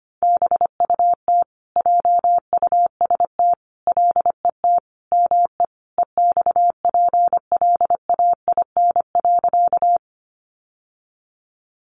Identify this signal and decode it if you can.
Morse code, heard as a keyed tone, about 25 wpm, 700 Hz. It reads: BUT JUST LET ME EXPLAIN.
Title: Sentences from Top 1100 Words 25wpm